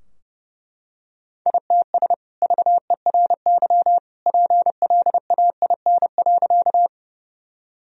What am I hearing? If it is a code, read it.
Morse code, heard as a keyed tone, about 30 wpm, 700 Hz. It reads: ITS VERY PLAIN.
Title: Sentences from Top 1000 Words - Review Part 2 - 30wpm